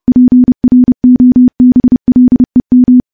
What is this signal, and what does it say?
Morse code, heard as a keyed tone, about 30 wpm, 250 Hz. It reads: PROBLEM
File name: morse / morse-training-30WPM